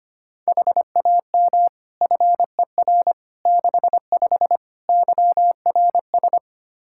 Morse code, heard as a keyed tone, about 25 wpm, 700 Hz. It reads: HAM FER 65 YRS
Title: Practice QSO - Exchange 3 - More Info - 25wpm